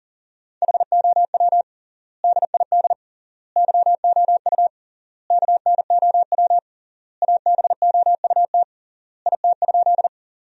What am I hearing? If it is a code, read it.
Morse code, heard as a keyed tone, about 40 wpm, 700 Hz. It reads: HOW DID YOU KNOW ABOUT IT?
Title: 3x Word-Spacing - Sentences from Top 100 Words - 40wpm